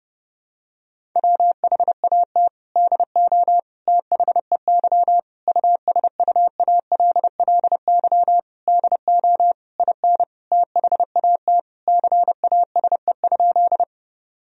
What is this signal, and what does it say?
Morse code, heard as a keyed tone, about 29 wpm, 700 Hz. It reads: WHAT DO THEY USUALLY DO IN THAT CASE?
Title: Sentences from Top 1000 Words - Review Part 3 - 30wpm